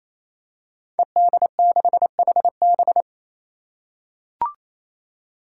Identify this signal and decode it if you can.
Morse code, heard as a keyed tone, about 27 wpm, 700 Hz. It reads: ED6HB
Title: Five Challenging Letter-Number Combinations 28wpm